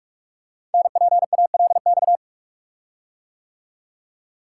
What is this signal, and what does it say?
Morse code, heard as a keyed tone, about 44 wpm, 700 Hz. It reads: NPALX
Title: Code Groups - 5 Letter:Numbers - 45wpm